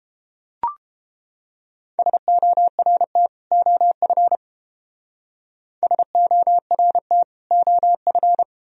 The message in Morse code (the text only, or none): SORT OF SORT OF